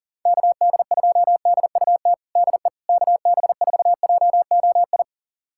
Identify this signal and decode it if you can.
Morse code, heard as a keyed tone, about 37 wpm, 700 Hz. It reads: KD2DUT DE KB4JOI
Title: USA Callsign DE Callsign - 6 Characters - 40wpm